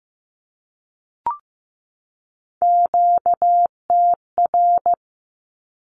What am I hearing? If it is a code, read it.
Morse code, heard as a keyed tone, about 15 wpm, 700 Hz. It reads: QTR
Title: Q-Codes 15wpm